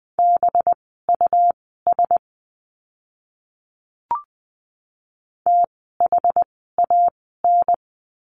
Morse code, heard as a keyed tone, about 20 wpm, spaced as for 14 wpm, 700 Hz. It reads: BUS THAN
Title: 185 Lesson 30f - Letter b - Words 20wpm